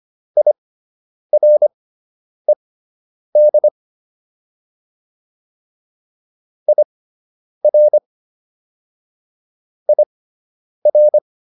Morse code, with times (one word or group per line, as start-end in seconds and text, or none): 0.4-3.7 IRED
6.7-8.0 IR
9.9-11.2 IR